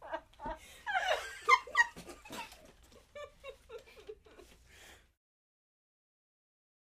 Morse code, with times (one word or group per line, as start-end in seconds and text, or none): none